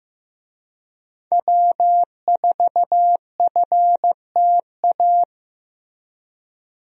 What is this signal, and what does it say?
Morse code, heard as a keyed tone, about 15 wpm, 700 Hz. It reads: W4FTA
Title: USA Call Signs - 5 Characters - Repeated 2x - 15wpm